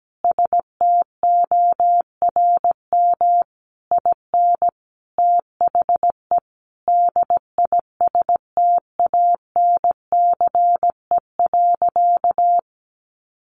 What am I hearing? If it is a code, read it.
Morse code, heard as a keyed tone, about 17 wpm, 700 Hz. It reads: STORM IN THE DISTANCE.
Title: Sentences from Top 1500 Words 17wpm